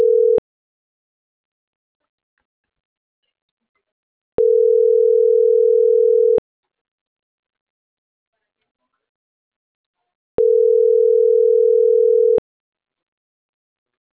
Ringback tone (North America), 440 + 480 Hz.